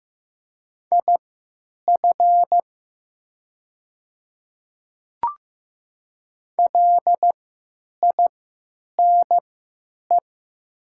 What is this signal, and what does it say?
Morse code, heard as a keyed tone, about 15 wpm, spaced as for 9 wpm, 700 Hz. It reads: IF LINE
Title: 3x Character Spacing - Top 300 Words - 15wpm